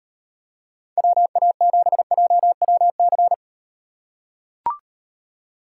Morse code, heard as a keyed tone, about 38 wpm, 700 Hz. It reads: WA7JWC